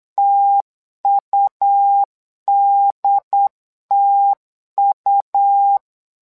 Morse code, heard as a keyed tone, about 8 wpm, 800 Hz. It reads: TUDTU
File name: LEZIONI / 2 UHDN / 2 UHD 8wpm 2258